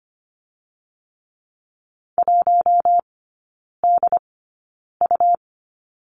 Morse code, heard as a keyed tone, about 25 wpm, spaced as for 9 wpm, 700 Hz. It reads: 1DU
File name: Lesson 17e - Letter u - Three Characters 25wpm